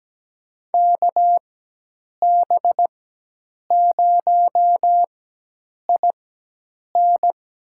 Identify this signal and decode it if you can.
Morse code, heard as a keyed tone, about 17 wpm, spaced as for 8 wpm, 700 Hz. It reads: KB0IN
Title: USA Call Signs - 6 Characters - 4x Character Spacing - 17wpm